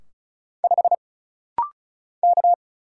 Morse code, heard as a keyed tone, about 36 wpm, 700 Hz. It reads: 5 K
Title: Single Letter-Number - Rapid-Fire 35wpm